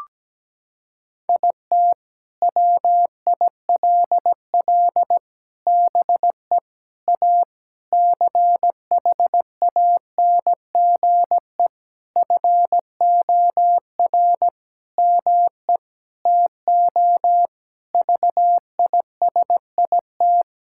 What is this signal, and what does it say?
Morse code, heard as a keyed tone, about 17 wpm, 700 Hz. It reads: IT WILL BE A CHANGE FOR ME TO VISIT
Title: Sentences from Top 1000 Words - Review Part 6 - 17wpm